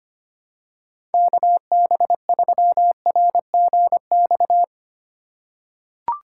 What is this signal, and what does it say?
Morse code, heard as a keyed tone, about 25 wpm, 700 Hz. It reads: KB3RGX
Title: USA Call Signs - 3x Character Spacing - 25wpm